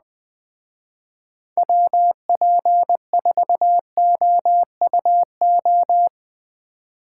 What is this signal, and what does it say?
Morse code, heard as a keyed tone, about 20 wpm, 700 Hz. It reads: WP4OUO
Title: Speed Racing - USA 2x3 Call Signs 17wpm